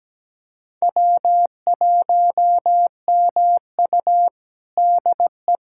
Morse code, heard as a keyed tone, about 17 wpm, 700 Hz. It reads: W1MU DE
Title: USA Callsign DE Callsign - 4 Characters - 17wpm